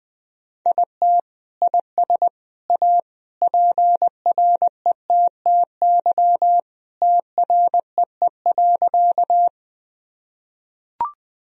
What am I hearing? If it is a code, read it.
Morse code, heard as a keyed tone, about 21 wpm, 700 Hz. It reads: IT IS A PRETTY TREE.
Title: Sentences from Top 1000 Words - Review Part 1 - 20wpm